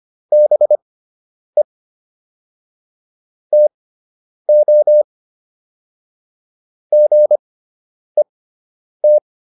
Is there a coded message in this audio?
Yes: Morse code from 0.3 to 9.2 s, heard as a keyed tone, about 25 wpm, spaced as for 9 wpm, 600 Hz. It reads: BE TO GET